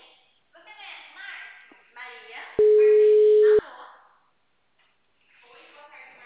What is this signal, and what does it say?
Ringback tone (Europe, 425 Hz), with speech.